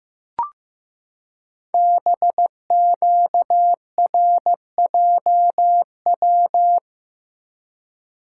Morse code, heard as a keyed tone, about 15 wpm, 700 Hz. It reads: BQRJW
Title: Code Groups - 5 Letters - 15wpm